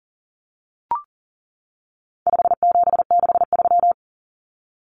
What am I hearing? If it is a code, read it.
Morse code, heard as a keyed tone, about 40 wpm, 700 Hz. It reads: 5763
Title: Common Cut Numbers - 4-Digit 40wpm